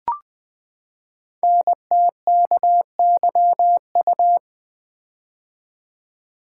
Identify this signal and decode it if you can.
Morse code, heard as a keyed tone, about 20 wpm, 700 Hz. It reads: NTKYU
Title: Code Groups - 5 Letters - 20wpm